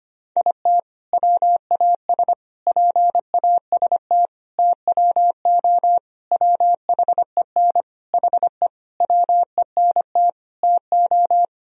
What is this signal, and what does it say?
Morse code, heard as a keyed tone, about 25 wpm, 700 Hz. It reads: IT WAS PAST TWO WHEN HE WENT TO
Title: Sentences from Top 1000 Words - Review Part 3 - 25wpm